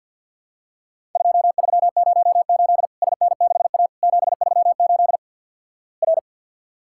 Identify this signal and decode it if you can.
Morse code, heard as a keyed tone, about 50 wpm, 700 Hz. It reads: 2308 SN6A 738 R
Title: Worked All Europe DX Contest - QTC Exchange with Repeats - 50wpm